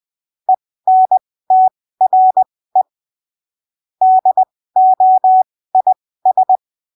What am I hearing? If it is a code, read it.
Morse code, heard as a keyed tone, about 20 wpm, spaced as for 15 wpm, 750 Hz. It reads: ENTRE DOIS
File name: arquivos / texto / texto 04 20ppm